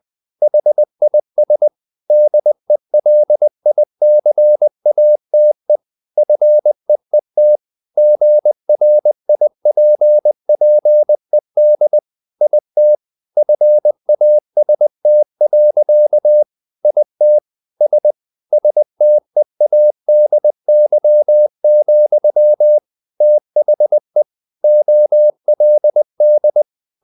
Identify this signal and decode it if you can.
Morse code, heard as a keyed tone, about 20 wpm, 600 Hz. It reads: HIS DELICATE FEET GRIPPED IT FAST. IT S STEADY, THE OLD
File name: the old man and the sea / the old man and the sea part 1 20wpm